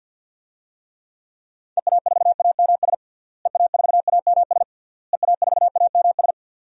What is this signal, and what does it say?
Morse code, heard as a keyed tone, about 47 wpm, 700 Hz. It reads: EA4AMS EA4AMS EA4AMS
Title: DXCC-100 Call Signs - 6 Characters - Repeated 3x - 50wpm